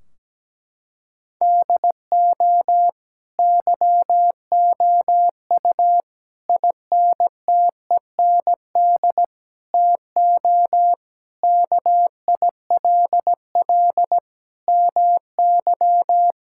Morse code, heard as a keyed tone, about 17 wpm, 700 Hz. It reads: DO YOU INTEND TO KILL MY
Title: Sentences from Top 1600 Words 17wpm